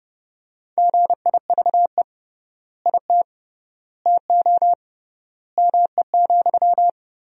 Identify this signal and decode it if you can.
Morse code, heard as a keyed tone, about 30 wpm, 700 Hz. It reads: GIVE IT TO ME,
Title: 3x Word-Spacing - Sentences from Top 200 Words - 30wpm